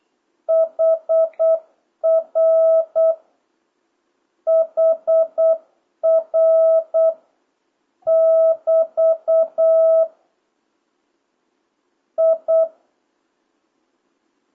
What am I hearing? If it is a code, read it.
Morse code, heard as a keyed tone, about 8 wpm, 1300 Hz, with noise. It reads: HR HR <BT> I
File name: jh8cbh / 2026cw